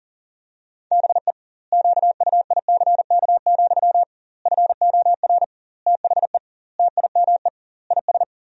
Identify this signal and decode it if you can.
Morse code, heard as a keyed tone, about 37 wpm, 700 Hz. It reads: BE QUICK, FOR THE TIME IS
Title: 2x Word-Spacing - Sentences from Top 500 Words - 40wpm